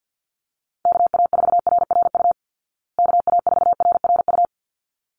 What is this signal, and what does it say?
Morse code, heard as a keyed tone, about 49 wpm, 700 Hz. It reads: KA4RRU KA4RRU
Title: Call Signs Repeated 3x - 50wpm